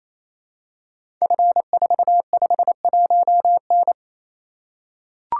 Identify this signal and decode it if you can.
Morse code, heard as a keyed tone, about 28 wpm, 700 Hz. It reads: F451N
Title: Code Groups - 5 Letter:Numbers - 28wpm